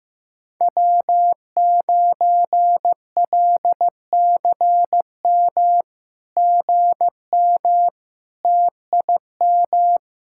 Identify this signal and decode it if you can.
Morse code, heard as a keyed tone, about 15 wpm, 700 Hz. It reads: W9LCM GM TIM
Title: SOTA QSO Practice 15wpm